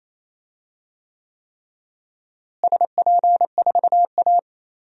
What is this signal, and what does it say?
Morse code, heard as a keyed tone, about 28 wpm, 700 Hz. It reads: SP4A
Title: DXCC-100 Call Signs - 4 Characters - Repeated 3x - 28wpm